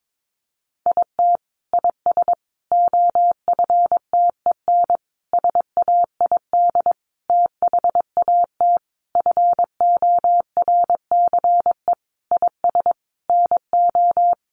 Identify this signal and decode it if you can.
Morse code, heard as a keyed tone, about 22 wpm, 700 Hz. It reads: IT IS OFTEN SAID THAT FORCE IS NO
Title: Sentences from Top 1400 Words 22wpm